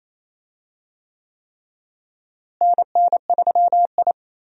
Morse code, heard as a keyed tone, about 28 wpm, 700 Hz. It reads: NN3I
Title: USA Call Signs - 4 Characters - Repeated 3x - 28wpm